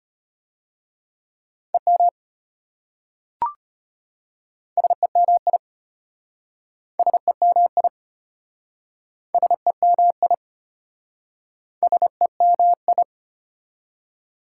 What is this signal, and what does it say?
Morse code, heard as a keyed tone, about 33 wpm, 700 Hz. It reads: EM SEMI SEMI SEMI SEMI